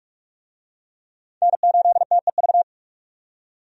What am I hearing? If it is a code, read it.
Morse code, heard as a keyed tone, about 46 wpm, 700 Hz. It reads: N8TEV
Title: USA Call Signs - 5 Characters - 45wpm